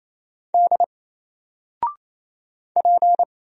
Morse code, heard as a keyed tone, about 27 wpm, 700 Hz. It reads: D P
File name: Single-Letter-Number-28wpm